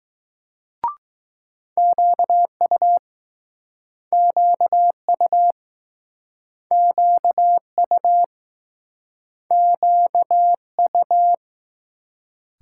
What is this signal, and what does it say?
Morse code, heard as a keyed tone, about 21 wpm, 700 Hz. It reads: QU QU QU QU